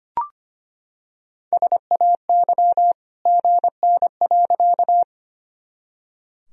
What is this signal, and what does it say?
Morse code, heard as a keyed tone, about 25 wpm, 700 Hz. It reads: SAY GN.
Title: Sentences from Top 300 Words and Abbreviations 25wpm